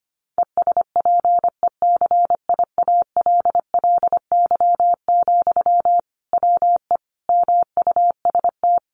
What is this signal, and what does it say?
Morse code, heard as a keyed tone, about 25 wpm, 700 Hz. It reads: ESPECIALLY, WE MUST
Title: Sentences from Top 700 Words 25wpm